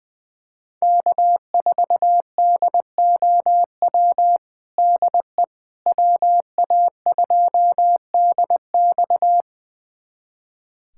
Morse code, heard as a keyed tone, about 20 wpm, 700 Hz. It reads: K4DOW DE WA2DX
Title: USA Callsign DE Callsign - 5 Characters - 20wpm